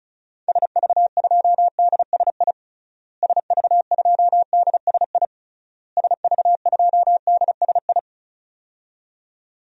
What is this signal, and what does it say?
Morse code, heard as a keyed tone, about 36 wpm, 700 Hz. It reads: SV2DSI SV2DSI SV2DSI
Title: DXCC-100 Call Signs - 6 Characters - Repeated 3x - 35wpm